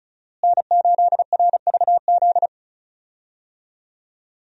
Morse code, heard as a keyed tone, about 34 wpm, 700 Hz. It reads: N8RVZ
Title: USA Call Signs - 5 Characters - Repeated 2x - 35wpm